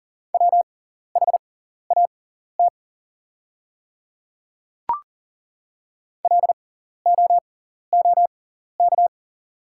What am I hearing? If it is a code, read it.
Morse code, heard as a keyed tone, about 40 wpm, spaced as for 14 wpm, 690 Hz. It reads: WHAT LOOK
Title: 6x Character Spacing - Top 100 Words - 40wpm